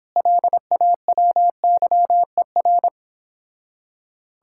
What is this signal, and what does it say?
Morse code, heard as a keyed tone, about 26 wpm, 700 Hz. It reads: LAWYER